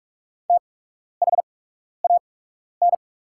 Morse code, heard as a keyed tone, about 44 wpm, spaced as for 12 wpm, 690 Hz. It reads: THAN
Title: 8x Character Spacing - Top 100 Words - 45wpm